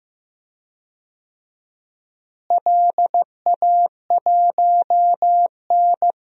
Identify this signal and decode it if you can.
Morse code, heard as a keyed tone, about 16 wpm, 700 Hz. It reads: LA1N
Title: DXCC-100 Call Signs - 4 Characters - Repeated 2x - 15wpm